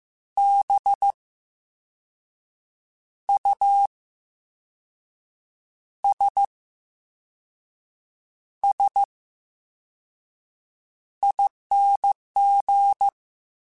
Morse code, heard as a keyed tone, about 15 wpm, spaced as for 5 wpm, 800 Hz. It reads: BUSSING
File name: Special INGs-mp3s / Special INGs-mp3s / VerySlowToFaster Ing @13